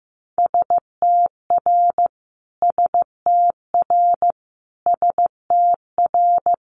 Morse code, heard as a keyed tone, about 15 wpm, 700 Hz. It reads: STR STR STR
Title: Multi-letter Phonemes Repeated 3x 15wpm